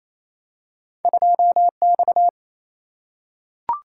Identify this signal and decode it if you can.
Morse code, heard as a keyed tone, about 28 wpm, 700 Hz. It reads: 2X
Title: Single-Letter Single-Number 28wpm